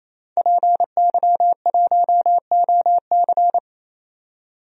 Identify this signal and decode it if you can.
Morse code, heard as a keyed tone, about 28 wpm, 700 Hz. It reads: PY1OC
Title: DXCC-100 Call Signs - 5 Characters - 28wpm